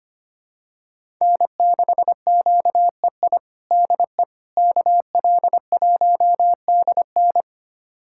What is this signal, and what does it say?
Morse code, heard as a keyed tone, about 25 wpm, 700 Hz. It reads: N6QEI DE KL1DN
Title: USA Callsign DE Callsign - 5 Characters - 25wpm